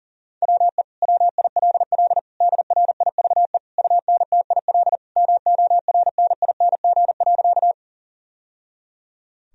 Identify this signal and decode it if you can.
Morse code, heard as a keyed tone, about 40 wpm, 700 Hz. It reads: WE WILL DRIVE UNTIL MORNING.
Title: Sentences from Top 1000 Words - Review Part 2 - 40wpm